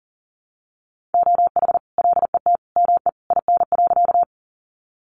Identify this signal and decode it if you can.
Morse code, heard as a keyed tone, about 40 wpm, 700 Hz. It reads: OH LET ME IN.
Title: Sentences from Top 500 Words 40wpm